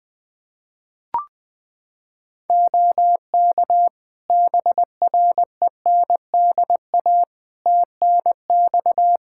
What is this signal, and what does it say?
Morse code, heard as a keyed tone, about 20 wpm, 700 Hz. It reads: OK BRENDA TNX
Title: Practice QSO - Exchange 4 - Ending - 20wpm